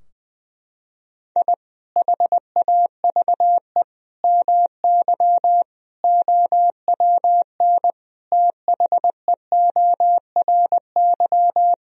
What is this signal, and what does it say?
Morse code, heard as a keyed tone, about 20 wpm, 700 Hz. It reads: I HAVE MY OWN THEORY
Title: Sentences from Top 1000 Words - Review Part 3 - 20wpm